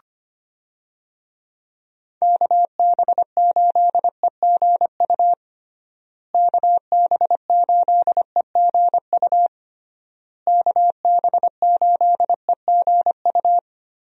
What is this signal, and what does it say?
Morse code, heard as a keyed tone, about 25 wpm, 700 Hz. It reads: KB8EGU KB8EGU KB8EGU
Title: USA Call Signs - 6 Characters - Repeated 3x - 25wpm